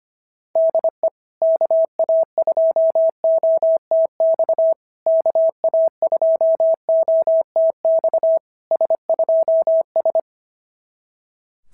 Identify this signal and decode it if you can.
Morse code, heard as a keyed tone, about 27 wpm, 650 Hz. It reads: DE KA2OTX KA2OTX S2S
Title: SOTA QSO Practice 25wpm